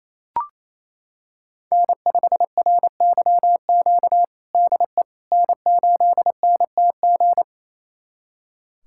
Morse code, heard as a keyed tone, about 28 wpm, 700 Hz. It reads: N5RYQ DE N8NTG
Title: USA Callsign DE Callsign - 5 Characters - 28wpm